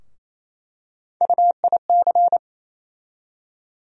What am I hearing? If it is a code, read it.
Morse code, heard as a keyed tone, about 28 wpm, 700 Hz. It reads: UIC
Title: USA Call Sign Postfixes - 3 Letters 28wpm